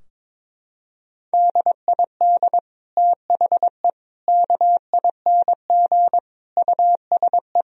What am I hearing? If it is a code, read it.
Morse code, heard as a keyed tone, about 22 wpm, 700 Hz. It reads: DID THE KING USE
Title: Sentences from Top 1000 Words - Review Part 4 - 22wpm